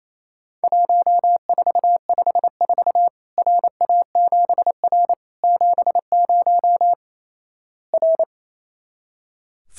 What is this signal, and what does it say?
Morse code, heard as a keyed tone, about 28 wpm, 700 Hz. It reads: 1454 RA7R 70 R
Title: Worked All Europe DX Contest - QTC Exchange - 28wpm